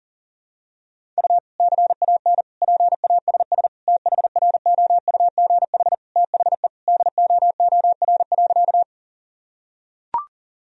Morse code, heard as a keyed tone, about 40 wpm, 700 Hz. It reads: U CAN PASS THROUGH THE DOOR.
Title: Sentences from Top 400 Words and Abbreviations 40wpm